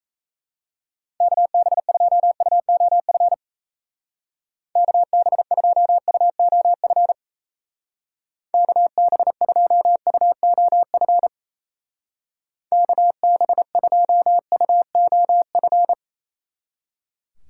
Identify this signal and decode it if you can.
Morse code, heard as a keyed tone, about 36 wpm, 700 Hz. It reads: KB2UOF KB2UOF KB2UOF KB2UOF